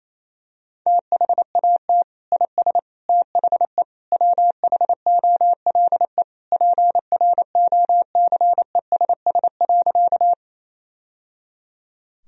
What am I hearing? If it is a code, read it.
Morse code, heard as a keyed tone, about 27 wpm, 700 Hz. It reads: THAT IS THE WHOLE PROCESS.
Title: Sentences from Top 1000 Words - Review Part 3 - 28wpm